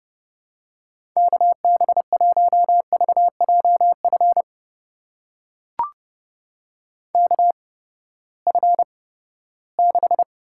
Morse code, heard as a keyed tone, about 30 wpm, spaced as for 17 wpm, 700 Hz. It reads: KB1VJF KF6
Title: USA Call Signs - 6 Characters - 8x Character Spacing - 30wpm